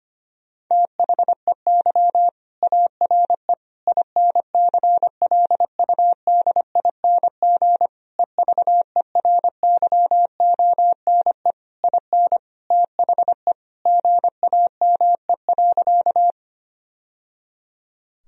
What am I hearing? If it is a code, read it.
Morse code, heard as a keyed tone, about 25 wpm, 700 Hz. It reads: THEY ARE INCLUDING EVERYONE IN THE GAME.
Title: Sentences from Top 1000 Words - Review Part 4 - 25wpm